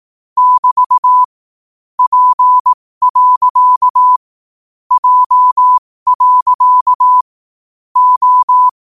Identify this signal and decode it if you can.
Morse code, heard as a keyed tone, about 18 wpm, spaced as for 15 wpm, 1000 Hz. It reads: <BT> P. J. O
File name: qotdfiles / 15wpm